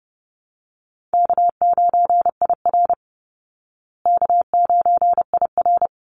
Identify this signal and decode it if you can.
Morse code, heard as a keyed tone, about 30 wpm, 700 Hz. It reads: K9IR K9IR
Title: Call Signs Repeated 3x - 30wpm